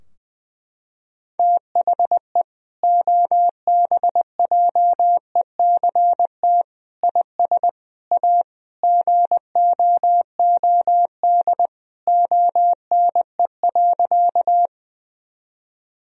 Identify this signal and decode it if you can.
Morse code, heard as a keyed tone, about 19 wpm, 700 Hz. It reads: THE OBJECT IS A GOOD ONE.
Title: Sentences from Top 1000 Words - Review Part 3 - 20wpm